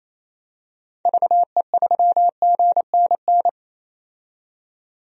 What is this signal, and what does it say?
Morse code, heard as a keyed tone, about 27 wpm, 700 Hz. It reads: VE3GNN
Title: DXCC-100 Call Signs - 6 Characters - 28wpm